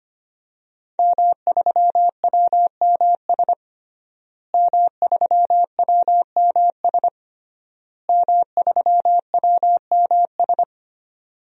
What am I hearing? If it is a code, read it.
Morse code, heard as a keyed tone, about 25 wpm, 700 Hz. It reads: M3WMS M3WMS M3WMS
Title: DXCC-100 Call Signs - 5 Characters - Repeated 3x - 25wpm